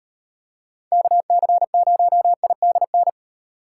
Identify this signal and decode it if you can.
Morse code, heard as a keyed tone, about 38 wpm, 700 Hz. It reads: KC0IDN